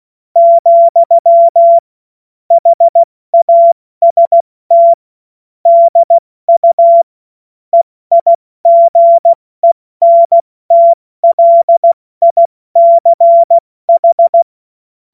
Morse code, heard as a keyed tone, about 16 wpm, 680 Hz. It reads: , HAST DU EIGENTLICH
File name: Morsekurs / Witze / Witz 28. März 2023 70 F80